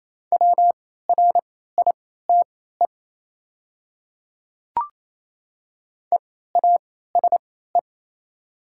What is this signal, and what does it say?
Morse code, heard as a keyed tone, about 27 wpm, spaced as for 16 wpm, 700 Hz. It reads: WRITE EASE
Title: 3x Character Spacing - Top 500 Words - 28wpm